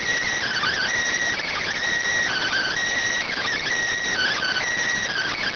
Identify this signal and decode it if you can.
Emergency Alert System (SAME) tones (header bursts), heard as harsh buzzing bursts. It is heard from 0.3 to 4.7 s.